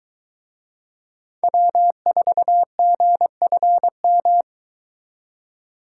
Morse code, heard as a keyed tone, about 23 wpm, 700 Hz. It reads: W4GFM